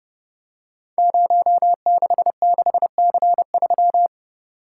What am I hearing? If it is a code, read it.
Morse code, heard as a keyed tone, about 30 wpm, 700 Hz. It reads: 066C3
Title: Code Groups - 5 Letter:Numbers - 30wpm